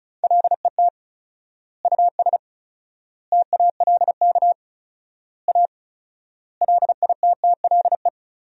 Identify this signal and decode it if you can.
Morse code, heard as a keyed tone, about 36 wpm, 700 Hz. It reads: LET US TALK A LITTLE
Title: 4x Word-Spacing - Sentences from Top 400 Words - 35wpm